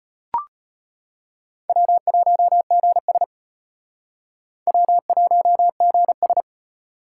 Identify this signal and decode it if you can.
Morse code, heard as a keyed tone, about 34 wpm, 700 Hz. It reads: W1GS W1GS